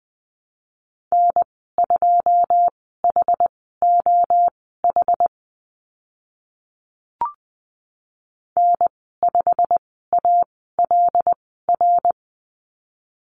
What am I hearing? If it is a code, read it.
Morse code, heard as a keyed tone, about 20 wpm, spaced as for 15 wpm, 700 Hz. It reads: N2HOH N5ALR
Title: 088 Lesson 15g - Number 5 - Call Signs 20wpm